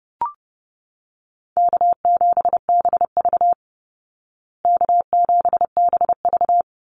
Morse code, heard as a keyed tone, about 30 wpm, 700 Hz. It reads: K7BV K7BV
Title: Call Signs Repeated 3x - 30wpm